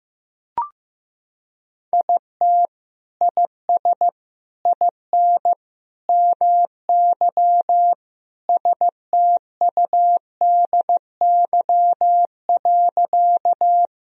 Morse code, heard as a keyed tone, about 15 wpm, 700 Hz. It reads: IT IS IN MY STUDY.
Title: Sentences from Top 1000 Words - Review Part 1 - 15wpm